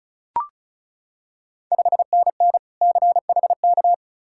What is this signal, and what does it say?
Morse code, heard as a keyed tone, about 36 wpm, 690 Hz. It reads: 5NN CHK
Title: Ontario QSO Party Exchange 35wpm